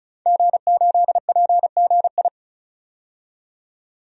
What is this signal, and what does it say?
Morse code, heard as a keyed tone, about 34 wpm, 700 Hz. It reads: G8PGI
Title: DXCC-100 Call Signs - 5 Characters - 35wpm